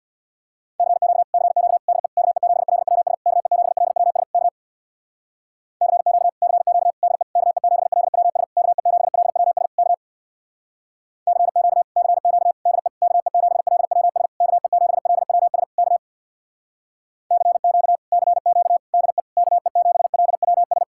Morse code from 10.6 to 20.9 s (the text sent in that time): CQ CQ DE KE7LWS KE7LWS K CQ CQ DE KE7LWS